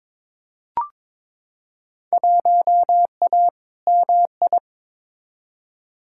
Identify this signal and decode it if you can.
Morse code, heard as a keyed tone, about 22 wpm, 700 Hz. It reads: 1A MI
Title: Field Day Exchanges - 22wpm